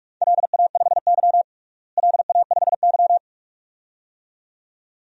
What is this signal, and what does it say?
Morse code, heard as a keyed tone, about 46 wpm, 700 Hz. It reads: LA5Y LA5Y
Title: DXCC-100 Call Signs - 4 Characters - Repeated 2x - 45wpm